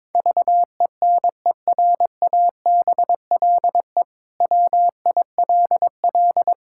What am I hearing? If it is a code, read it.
Morse code, heard as a keyed tone, about 21 wpm, 700 Hz. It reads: VENERABLE WILL
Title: Pangram Challenge v2 - 22wpm